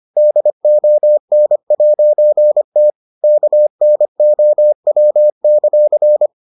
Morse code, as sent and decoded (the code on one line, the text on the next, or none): -.. --- -. .----. - / -.- -. --- .-- -.-.-.
DON'T KNOW;